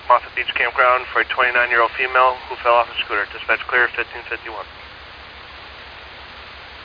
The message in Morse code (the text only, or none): none